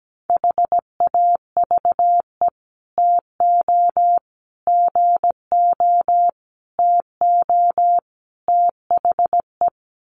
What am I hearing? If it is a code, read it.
Morse code, heard as a keyed tone, about 17 wpm, 700 Hz. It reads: HAVE TO GO TO THE
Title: Sets of 5 Words Encore 17wpm